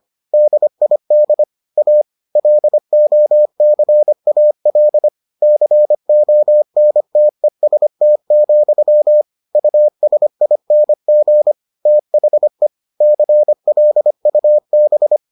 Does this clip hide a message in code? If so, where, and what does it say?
Yes: Morse code from 0.3 to 15.2 s, heard as a keyed tone, about 25 wpm, 600 Hz. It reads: DID A LOCAL CONTEST, USING THE CLUB